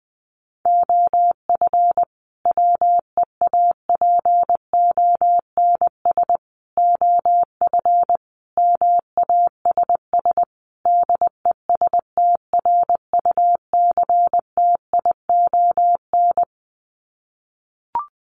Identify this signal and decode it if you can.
Morse code, heard as a keyed tone, about 20 wpm, 700 Hz. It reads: OF WEAPONS OF MASS DESTRUCTION
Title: Sets of 5 Words Encore 20wpm